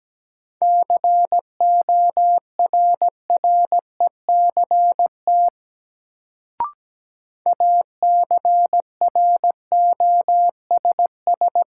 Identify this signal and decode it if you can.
Morse code, heard as a keyed tone, about 17 wpm, 700 Hz. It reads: CORRECT ACROSS
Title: Sentences from Top 700 Words - New Word Drill - 17wpm